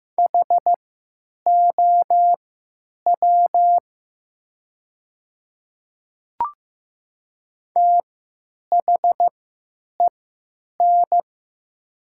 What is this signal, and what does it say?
Morse code, heard as a keyed tone, about 15 wpm, spaced as for 9 wpm, 700 Hz. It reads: HOW THEN
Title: 3x Character Spacing - Top 100 Words - 15wpm